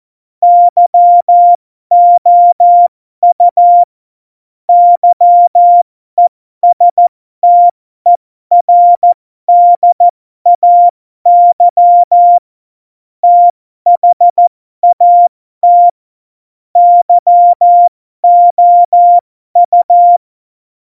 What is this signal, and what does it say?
Morse code, heard as a keyed tone, about 14 wpm, spaced as for 12 wpm, 700 Hz. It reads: YOU YESTERDAY THAT YOU